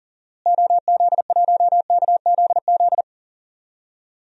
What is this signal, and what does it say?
Morse code, heard as a keyed tone, about 40 wpm, 700 Hz. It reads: OZ1KZZ
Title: DXCC-100 Call Signs - 6 Characters - Repeated 2x - 40wpm